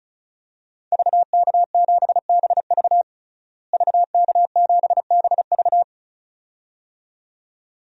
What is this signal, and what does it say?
Morse code, heard as a keyed tone, about 34 wpm, 700 Hz. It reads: VK7BV VK7BV
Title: DXCC-100 Call Signs - 5 Characters - Repeated 2x - 35wpm